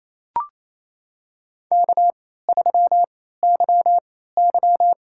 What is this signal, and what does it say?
Morse code, heard as a keyed tone, about 27 wpm, spaced as for 16 wpm, 700 Hz. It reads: K3YY
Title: USA Call Signs - 4 Characters - 3x Character Spacing - 28wpm